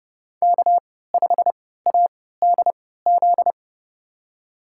Morse code, heard as a keyed tone, about 30 wpm, spaced as for 17 wpm, 700 Hz. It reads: K5ADZ
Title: USA Call Signs - 5 Characters - 3x Character Spacing - 30wpm